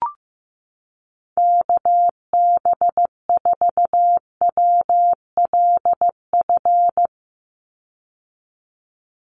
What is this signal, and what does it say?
Morse code, heard as a keyed tone, about 15 wpm, 700 Hz. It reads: KB4WLF
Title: Call Signs Repeated 3x - 15wpm